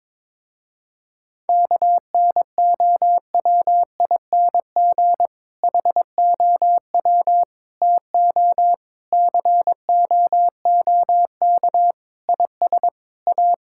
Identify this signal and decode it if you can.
Morse code, heard as a keyed tone, about 22 wpm, 700 Hz. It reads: KNOWING HOW TO COOK IS A
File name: Difficult Sentences with Hints - 22wpm